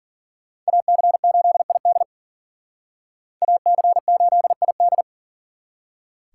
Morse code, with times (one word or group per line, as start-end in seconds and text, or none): 0.7-2.0 AC8ID
3.4-5.0 AC8ID